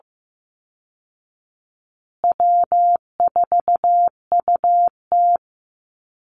Morse code, heard as a keyed tone, about 15 wpm, 700 Hz. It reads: W4UT
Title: Call Signs Repeated 3x - 15wpm